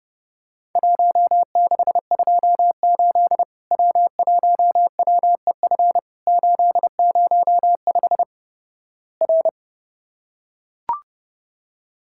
Morse code, heard as a keyed tone, about 32 wpm, 700 Hz. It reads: 1628 W1WEF 805 R
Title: Worked All Europe DX Contest - QTC Exchange - 30wpm